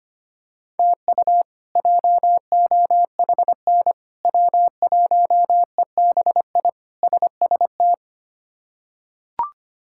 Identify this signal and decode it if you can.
Morse code, heard as a keyed tone, about 25 wpm, 700 Hz. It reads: TU JOHN W1EBI SST
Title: K1USN SST QSO Practice - 25wpm